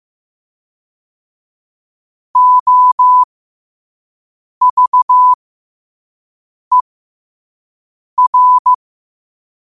Morse code, heard as a keyed tone, about 15 wpm, spaced as for 5 wpm, 1000 Hz. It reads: OVER
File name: qotdfiles / 5wpm